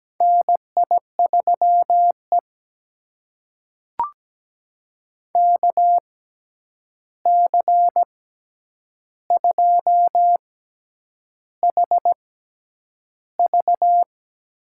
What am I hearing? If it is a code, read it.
Morse code, heard as a keyed tone, about 17 wpm, spaced as for 8 wpm, 700 Hz. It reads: NI3E KC2HV
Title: USA Call Signs - 6x Character Spacing - 17wpm